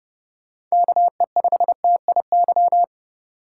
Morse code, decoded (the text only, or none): KE5TIY